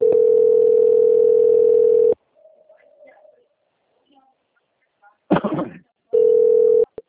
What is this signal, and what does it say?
Ringback tone (North America), 440 + 480 Hz.
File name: out- - - - - .56223